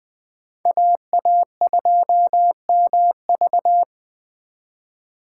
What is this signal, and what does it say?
Morse code, heard as a keyed tone, about 24 wpm, 700 Hz. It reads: AA2MV
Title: Speed Racing - USA 2x2 Call Signs 20wpm